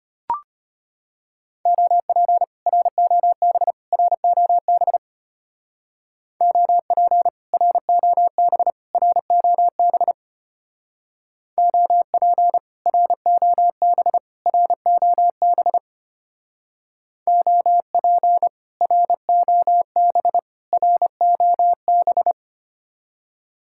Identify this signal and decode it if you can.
Morse code, heard as a keyed tone, about 30 wpm, 700 Hz. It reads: OP ROB ROB OP ROB ROB OP ROB ROB OP ROB ROB
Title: Speed-Racing Common QSO Phrases - 25wpm